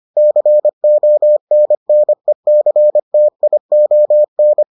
Morse code, heard as a keyed tone, about 25 wpm, 600 Hz. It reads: CONNECTION